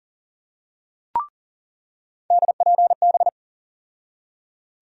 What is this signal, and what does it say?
Morse code, heard as a keyed tone, about 40 wpm, 700 Hz. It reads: DPB
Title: USA Call Sign Postfixes - 3 Letters 40wpm